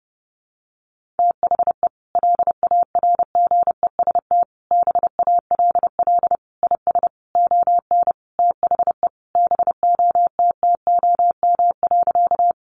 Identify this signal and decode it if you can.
Morse code, heard as a keyed tone, about 30 wpm, 700 Hz. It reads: THE LARGEST BALL IS ON THE BOTTOM.
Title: Easy Phrases 30wpm